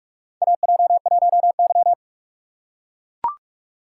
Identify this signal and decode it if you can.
Morse code, heard as a keyed tone, about 43 wpm, 700 Hz. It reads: AJ1Y
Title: USA Call Signs - 4 Characters - 3x Character Spacing - 45wpm